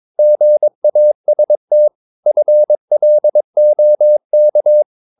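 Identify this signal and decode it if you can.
Morse code, heard as a keyed tone, about 22 wpm, 600 Hz. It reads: GAST FLOK
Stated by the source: Morse practice of short Danish words at 22 wpm, from CW med Gnister.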